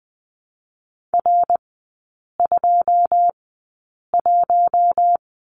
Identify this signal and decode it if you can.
Morse code, heard as a keyed tone, about 19 wpm, spaced as for 8 wpm, 700 Hz. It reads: R21
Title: 107 Lesson 18e - Letter m - Three Characters 20wpm